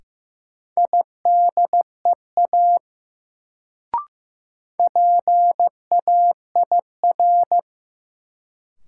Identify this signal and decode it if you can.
Morse code, heard as a keyed tone, about 15 wpm, 700 Hz. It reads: IDEA PAIR
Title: Sets of 3 Words - New Word Drill - 15wpm